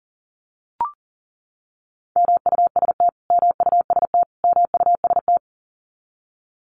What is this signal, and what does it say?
Morse code, heard as a keyed tone, about 40 wpm, 700 Hz. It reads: MUST MUST MUST
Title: Top 200 Words Repeated 40wpm